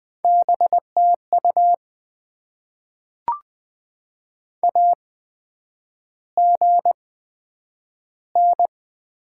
Morse code, heard as a keyed tone, about 20 wpm, spaced as for 9 wpm, 700 Hz. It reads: BTU AGN
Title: Top 20 QSO Elements - 8x Character Spacing - 20wpm